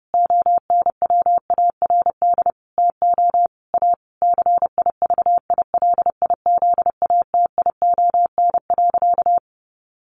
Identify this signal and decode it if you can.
Morse code, heard as a keyed tone, about 30 wpm, 700 Hz. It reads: ONWARD TO A CIVILIZATION.
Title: A Sound Of Thunder 30wpm